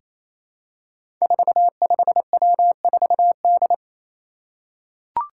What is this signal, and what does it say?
Morse code, heard as a keyed tone, about 27 wpm, 700 Hz. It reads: 45W4D
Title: Five Challenging Letter-Number Combinations 28wpm